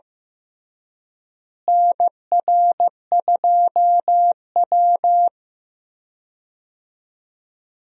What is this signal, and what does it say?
Morse code, heard as a keyed tone, about 15 wpm, 700 Hz. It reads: NR2W
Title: USA Call Signs - 4 Characters - Repeated 3x - 15wpm